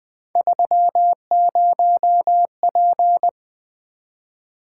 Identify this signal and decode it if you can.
Morse code, heard as a keyed tone, about 24 wpm, 700 Hz. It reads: 30P
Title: Speed-Racing - Three Letter-Number Combinations - 20wpm